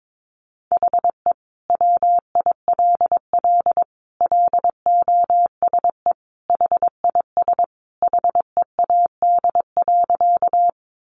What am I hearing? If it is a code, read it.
Morse code, heard as a keyed tone, about 22 wpm, 700 Hz. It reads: HE WILL LOSE HIS HEAD.
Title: Sentences from Top 1100 Words 22wpm